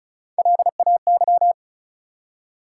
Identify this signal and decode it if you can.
Morse code, heard as a keyed tone, about 36 wpm, 700 Hz. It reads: LAY
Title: Three Letter Words - 35wpm